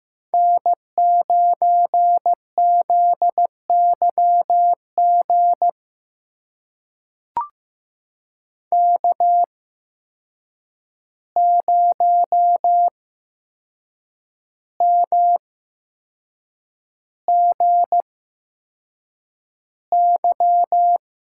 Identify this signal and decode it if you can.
Morse code, heard as a keyed tone, about 15 wpm, spaced as for 6 wpm, 700 Hz. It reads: N9ZYG K0MGY
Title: USA Call Signs - 5 Characters - 8x Character Spacing - 15wpm